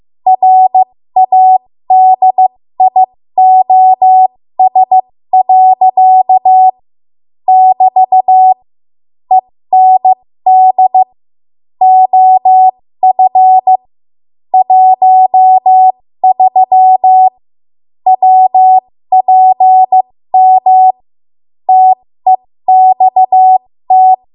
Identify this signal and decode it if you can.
Morse code, heard as a keyed tone, about 15 wpm, spaced as for 13 wpm, 750 Hz. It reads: RADIOS. <BT> END OF 13 WPM TEXT
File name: Morse / 13WPM